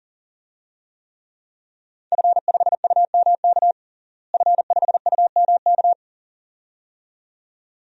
Morse code, heard as a keyed tone, about 37 wpm, 700 Hz. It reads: F5UMK F5UMK
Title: DXCC-100 Call Signs - 5 Characters - Repeated 2x - 40wpm